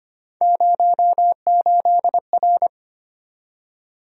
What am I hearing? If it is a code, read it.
Morse code, heard as a keyed tone, about 25 wpm, 700 Hz. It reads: 08R
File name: Three Letter-Number Combinations - 25wpm